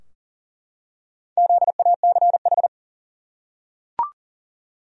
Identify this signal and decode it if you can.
Morse code, heard as a keyed tone, about 39 wpm, 700 Hz. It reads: ZACH
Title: USA Common Names - Max 5 Letters - 40wpm